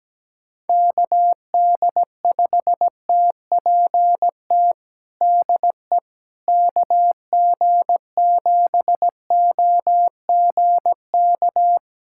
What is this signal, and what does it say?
Morse code, heard as a keyed tone, about 17 wpm, 700 Hz. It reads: KD5TPT DE KG7OGK
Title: USA Callsign DE Callsign 17wpm